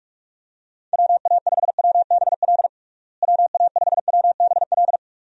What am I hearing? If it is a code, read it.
Morse code, heard as a keyed tone, about 41 wpm, 700 Hz. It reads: WA5WBL WA5WBL
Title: USA Call Signs - 6 Characters - Repeated 3x - 45wpm